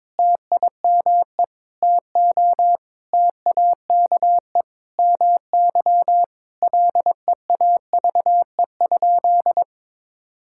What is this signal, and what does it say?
Morse code, heard as a keyed tone, about 22 wpm, 700 Hz. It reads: TIME TO TAKE MY LEAVE?
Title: Sentences from Top 1000 Words - Review Part 5 - 22wpm